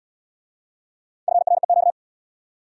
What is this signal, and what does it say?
Morse code, heard as a keyed tone, about 68 wpm, 700 Hz. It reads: FREQ